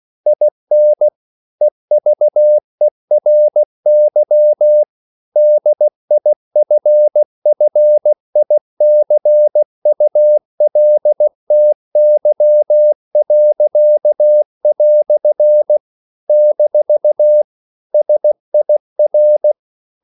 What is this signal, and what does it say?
Morse code, heard as a keyed tone, about 16 wpm, 600 Hz. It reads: IN EVERY DIFFICULTY." - SIR
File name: CW / Practice / EnglishQuotations0416